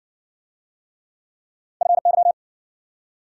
Morse code, heard as a keyed tone, about 57 wpm, 700 Hz. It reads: FY